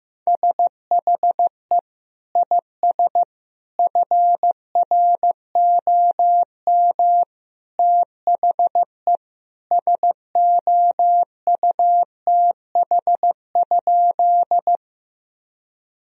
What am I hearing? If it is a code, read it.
Morse code, heard as a keyed tone, about 15 wpm, 700 Hz. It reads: SHE IS FROM THE SOUTH?
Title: Sentences from Top 1000 Words - Review Part 2 - 15wpm